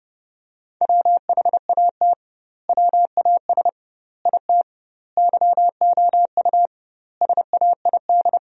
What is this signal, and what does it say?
Morse code, heard as a keyed tone, about 30 wpm, 700 Hz. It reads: WHAT WAS IT YOU SAID
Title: 2x Word-Spacing - Sentences from Top 200 Words - 30wpm